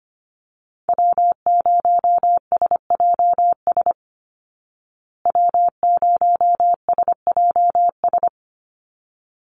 Morse code, heard as a keyed tone, about 25 wpm, 700 Hz. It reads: W0SJS W0SJS
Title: Call Signs Repeated 3x - 25wpm